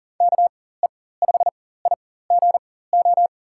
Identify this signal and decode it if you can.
Morse code, heard as a keyed tone, about 38 wpm, spaced as for 18 wpm, 700 Hz. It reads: KE5IGO
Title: USA Call Signs - 6 Characters - 4x Character Spacing - 40wpm